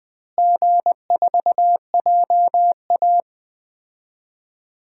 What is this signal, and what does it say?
Morse code, heard as a keyed tone, about 20 wpm, 700 Hz. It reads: G4JA
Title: DXCC-100 Call Signs - 4 Characters - Repeated 2x - 20wpm